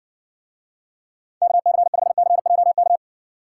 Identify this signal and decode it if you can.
Morse code, heard as a keyed tone, about 60 wpm, 700 Hz. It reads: KC5CJK